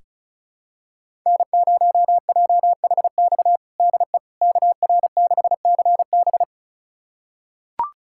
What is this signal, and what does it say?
Morse code, heard as a keyed tone, about 36 wpm, 700 Hz. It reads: N0JHX DE KR6CB
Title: USA Callsign DE Callsign - 5 Characters - 35wpm